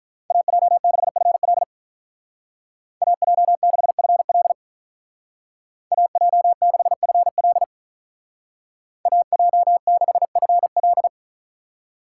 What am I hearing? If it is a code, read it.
Morse code, heard as a keyed tone, about 44 wpm, 700 Hz. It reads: AJ6FL AJ6FL AJ6FL AJ6FL
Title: Speed Racing - USA 2x2 Call Signs 35wpm